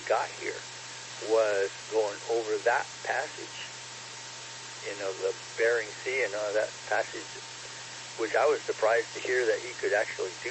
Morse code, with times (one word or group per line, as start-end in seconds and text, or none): none